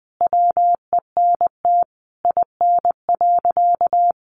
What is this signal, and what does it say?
Morse code, heard as a keyed tone, about 19 wpm, 700 Hz. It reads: WENT IN.
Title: Sentences from Top 300 Words 20wpm